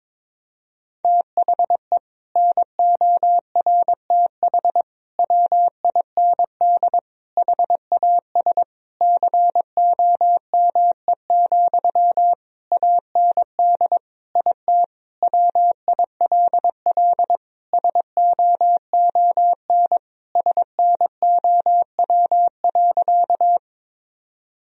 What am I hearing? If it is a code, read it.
Morse code, heard as a keyed tone, about 22 wpm, 700 Hz. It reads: THE NORTH WIND HAS COME, AND IT WILL SOON SNOW.
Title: Sentences from Top 1000 Words - Review Part 3 - 22wpm